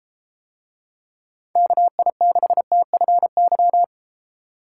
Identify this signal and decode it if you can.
Morse code, heard as a keyed tone, about 31 wpm, 700 Hz. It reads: KI6TFY